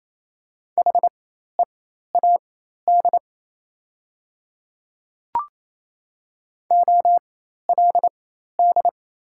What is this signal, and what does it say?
Morse code, heard as a keyed tone, about 28 wpm, spaced as for 13 wpm, 690 Hz. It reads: HEAD OLD
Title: 4x Character Spacing - Top 300 Words - 28wpm